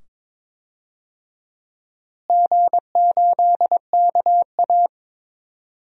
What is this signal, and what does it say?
Morse code, heard as a keyed tone, about 22 wpm, 700 Hz. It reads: G8KA
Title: DXCC-100 Call Signs - 4 Characters - Repeated 2x - 22wpm